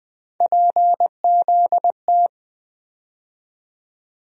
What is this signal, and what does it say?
Morse code, heard as a keyed tone, about 20 wpm, 700 Hz. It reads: PZT
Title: USA Call Sign Postfixes - 3 Letters 20wpm